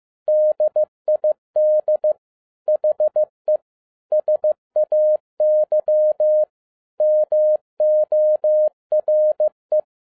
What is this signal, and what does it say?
Morse code, heard as a keyed tone, about 15 wpm, 610 Hz. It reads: DID HE SAY MORE